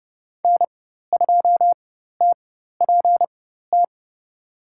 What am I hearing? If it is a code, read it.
Morse code, heard as a keyed tone, about 32 wpm, spaced as for 14 wpm, 700 Hz. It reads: N2TPT
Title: USA Call Signs - 5 Characters - 4x Character Spacing - 30wpm